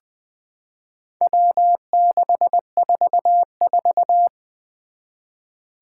Morse code, heard as a keyed tone, about 20 wpm, 700 Hz. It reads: W644
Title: Four Challenging Letter-Number Combinations 20wpm